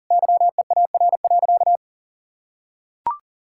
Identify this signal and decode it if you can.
Morse code, heard as a keyed tone, about 40 wpm, 700 Hz. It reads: YEAR.
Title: Sentences from Top 1000 Words - Review Part 5 - 40wpm